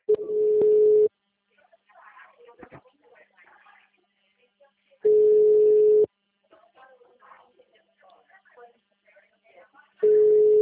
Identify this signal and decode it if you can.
Ringback tone (Europe, 425 Hz), with speech.